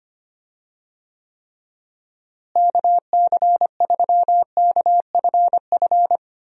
Morse code, heard as a keyed tone, about 25 wpm, 700 Hz. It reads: KC3KFF
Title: USA Call Signs - 6 Characters - Repeated 2x - 25wpm